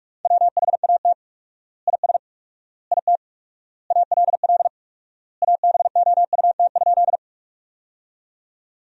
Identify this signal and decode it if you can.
Morse code, heard as a keyed tone, about 46 wpm, 700 Hz. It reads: WHAT IS IT ALL ABOUT?
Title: 4x Word-Spacing - Sentences from Top 100 Words - 45wpm